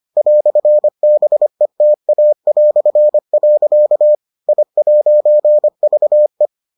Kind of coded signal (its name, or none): Morse code